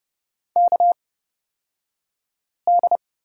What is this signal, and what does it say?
Morse code, heard as a keyed tone, about 30 wpm, spaced as for 5 wpm, 700 Hz. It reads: KD
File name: Lesson 38d - Letter q - Two Characters 30wpm